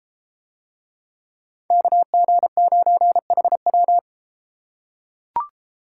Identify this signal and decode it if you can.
Morse code, heard as a keyed tone, about 32 wpm, 700 Hz. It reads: KG9HW